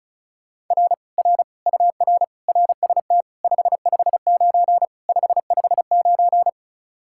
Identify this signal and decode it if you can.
Morse code, heard as a keyed tone, about 43 wpm, 700 Hz. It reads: R R UR RST 559 559
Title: Speed-Racing Common QSO Phrases - 35wpm